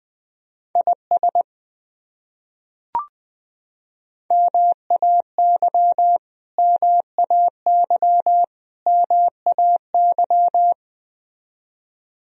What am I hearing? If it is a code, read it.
Morse code, heard as a keyed tone, about 20 wpm, 700 Hz. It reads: IS MAY MAY MAY
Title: Top 100 Words Repeated 20wpm